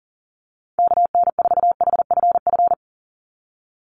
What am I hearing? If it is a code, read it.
Morse code, heard as a keyed tone, about 40 wpm, 700 Hz. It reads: KN4HFF
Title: USA Call Signs 40wpm